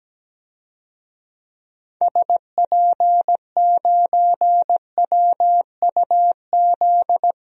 Morse code, heard as a keyed tone, about 17 wpm, 700 Hz. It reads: SP9WUZ
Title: DXCC-100 Call Signs - 6 Characters - 17wpm